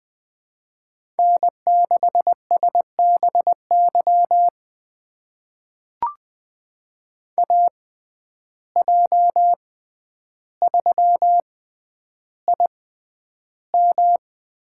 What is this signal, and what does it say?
Morse code, heard as a keyed tone, about 20 wpm, spaced as for 10 wpm, 700 Hz. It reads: N6SBY AJ3IM
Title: USA Call Signs - 5 Characters - 6x Character Spacing - 20wpm